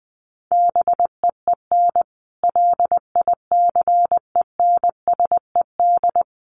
Morse code, heard as a keyed tone, about 20 wpm, 700 Hz. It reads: BEEN LICENSED
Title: Common QSO Phrases 20wpm